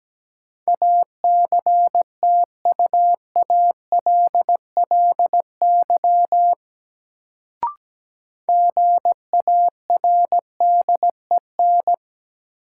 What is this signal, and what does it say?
Morse code, heard as a keyed tone, about 17 wpm, 700 Hz. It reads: ACTUALLY GARDEN
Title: Sentences from Top 700 Words - New Word Drill - 17wpm